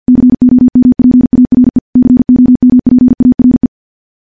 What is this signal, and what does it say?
Morse code, heard as a keyed tone, about 50 wpm, 250 Hz. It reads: COMPARE COMPARE
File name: morse-training-50WPM